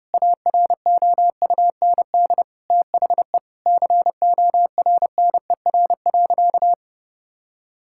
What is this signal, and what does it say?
Morse code, heard as a keyed tone, about 30 wpm, 700 Hz. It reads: AROUND THE CORNER.
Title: Sentences from Top 1600 Words 30wpm